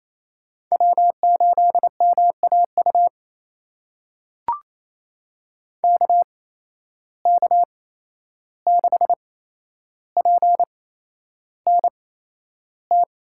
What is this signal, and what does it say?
Morse code, heard as a keyed tone, about 28 wpm, spaced as for 11 wpm, 700 Hz. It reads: W8MAU KK6PNT
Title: USA Call Signs - 8x Character Spacing - 28wpm